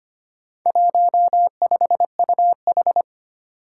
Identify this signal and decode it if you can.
Morse code, heard as a keyed tone, about 25 wpm, 700 Hz. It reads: 15UH
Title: Four Challenging Letter-Number Combinations 25wpm